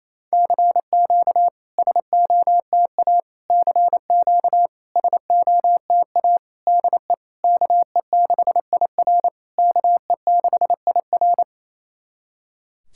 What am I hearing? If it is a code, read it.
Morse code, heard as a keyed tone, about 28 wpm, 690 Hz. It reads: CQ SOTA CQ SOTA DE KE6IR KE6IR
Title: SOTA QSO Practice 28wpm